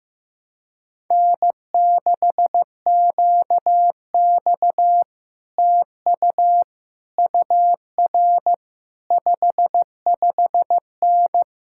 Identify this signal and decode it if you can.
Morse code, heard as a keyed tone, about 15 wpm, 700 Hz. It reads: N6QX TU UR 55N
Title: Speed-Racing Common QSO Phrases Encore - 15wpm